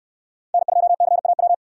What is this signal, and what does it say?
Morse code, heard as a keyed tone, about 68 wpm, 700 Hz. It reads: N2CAW